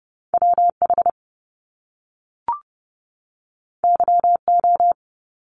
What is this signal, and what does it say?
Morse code, heard as a keyed tone, about 30 wpm, 700 Hz. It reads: WH YO
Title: Top Two Letter Combinations in Words 30wpm